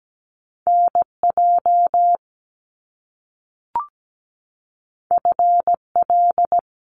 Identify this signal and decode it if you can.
Morse code, heard as a keyed tone, about 17 wpm, 700 Hz. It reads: NJ FL
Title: US States 17wpm